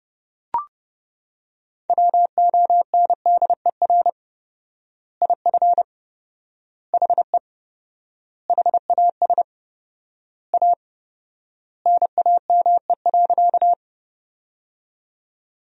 Morse code, heard as a keyed tone, about 31 wpm, 700 Hz. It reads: WONDER IF HE HAS A NAME.
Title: 4x Word-Spacing - Sentences from Top 500 Words - 30wpm